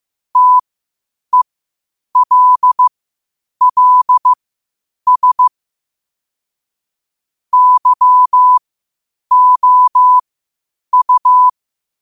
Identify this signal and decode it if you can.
Morse code, heard as a keyed tone, about 15 wpm, spaced as for 8 wpm, 1000 Hz. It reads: TELLS YOU
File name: qotdfiles / 8wpm